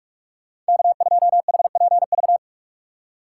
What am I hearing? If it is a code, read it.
Morse code, heard as a keyed tone, about 59 wpm, 700 Hz. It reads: K2HPV